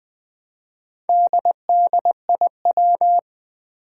Morse code, heard as a keyed tone, about 20 wpm, 700 Hz. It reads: DDIW
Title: Four Challenging Letter-Number Combinations 20wpm